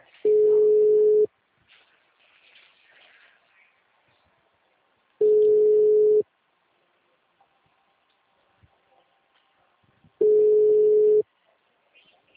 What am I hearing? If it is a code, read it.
Ringback tone (Europe, 425 Hz).